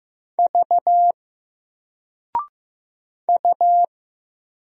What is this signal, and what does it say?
Morse code, heard as a keyed tone, about 15 wpm, 700 Hz. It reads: V U
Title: Single Letter-Number Alt 15wpm